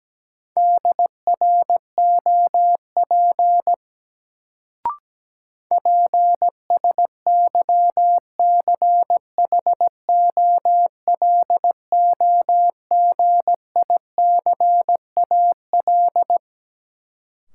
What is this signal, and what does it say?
Morse code, heard as a keyed tone, about 17 wpm, 700 Hz. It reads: DROP PSYCHOLOGICAL
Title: Sentences from Top 1400 Words - New Word Drill - 17wpm